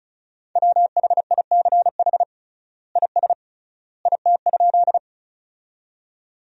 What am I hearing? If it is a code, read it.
Morse code, heard as a keyed tone, about 34 wpm, 700 Hz. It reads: WHICH IS IT?
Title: 3x Word-Spacing - Sentences from Top 100 Words - 35wpm